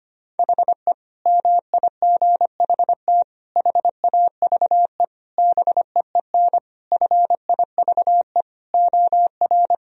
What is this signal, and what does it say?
Morse code, heard as a keyed tone, about 25 wpm, 700 Hz. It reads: HE MIGHT HAVE BEEN FIVE OR
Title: Sentences from Top 1000 Words - Review Part 2 - 25wpm